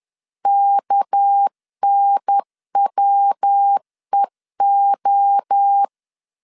Morse code, heard as a keyed tone, about 10 wpm, 800 Hz. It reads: KNWEO
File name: LEZIONI / 6 PKX0 / 6 PK 10wpm 2436